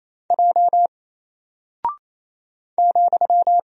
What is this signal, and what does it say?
Morse code, heard as a keyed tone, about 28 wpm, 700 Hz. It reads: J ,
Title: Single Character (Letter:Number:Punctuation:Prosign) 28wpm